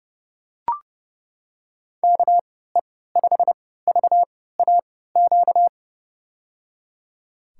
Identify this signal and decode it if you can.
Morse code, heard as a keyed tone, about 30 wpm, spaced as for 17 wpm, 700 Hz. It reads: KE5VAQ
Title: USA Call Signs - 6 Characters - 3x Character Spacing - 30wpm